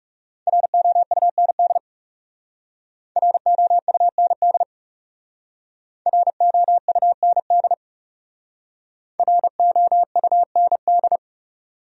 Morse code, heard as a keyed tone, about 36 wpm, 700 Hz. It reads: ROUND ROUND ROUND ROUND
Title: Speed-Racing Top 300 Words 30wpm